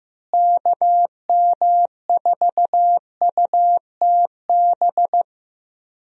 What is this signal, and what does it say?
Morse code, heard as a keyed tone, about 15 wpm, 700 Hz. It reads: KM4UTB
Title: USA Call Signs - 6 Characters - Repeated 2x - 15wpm